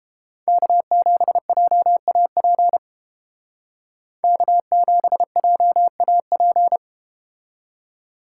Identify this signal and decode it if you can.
Morse code, heard as a keyed tone, about 31 wpm, 700 Hz. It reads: K7JAP K7JAP